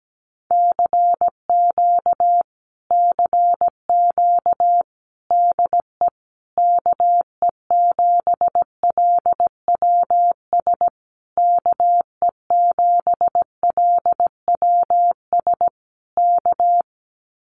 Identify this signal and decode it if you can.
Morse code, heard as a keyed tone, about 17 wpm, 700 Hz. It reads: CQ CQ DE KE7LWS KE7LWS K
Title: Common QSO Phrases Encore 17wpm